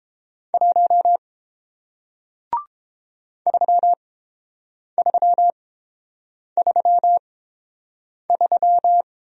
Morse code, heard as a keyed tone, about 30 wpm, spaced as for 7 wpm, 700 Hz. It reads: 1 3333